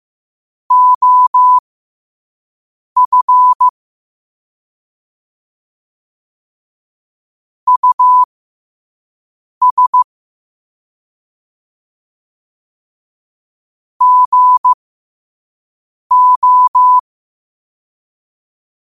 Morse code, heard as a keyed tone, about 15 wpm, spaced as for 5 wpm, 1000 Hz. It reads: OF US GO
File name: qotdfiles / 5wpm